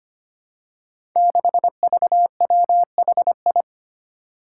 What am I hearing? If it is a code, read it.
Morse code, heard as a keyed tone, about 25 wpm, 700 Hz. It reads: 6VWHI
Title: Five Challenging Letter-Number Combinations 25wpm